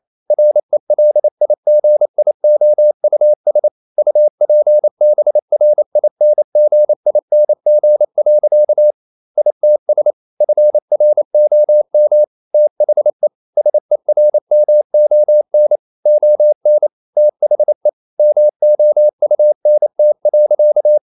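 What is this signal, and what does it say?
Morse code, heard as a keyed tone, about 28 wpm, 600 Hz. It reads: RELIGIOUS UPBRINGING. ITS FROM THE SERMON ON THE MOUNT.
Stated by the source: Morse code practice at 28 wpm, text of Ken Follett's novel Never in English, from CW med Gnister.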